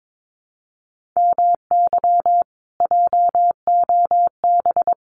Morse code, heard as a keyed tone, about 22 wpm, 700 Hz. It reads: MY JOB
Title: Common QSO Phrases Encore 22wpm